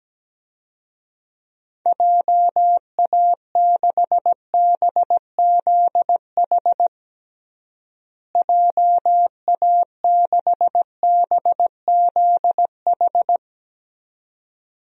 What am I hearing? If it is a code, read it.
Morse code, heard as a keyed tone, about 17 wpm, 700 Hz. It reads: JA6BZH JA6BZH
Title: DXCC-100 Call Signs - 6 Characters - Repeated 2x - 17wpm